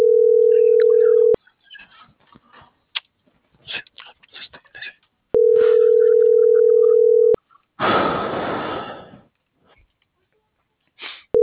Ringback tone (North America), 440 + 480 Hz.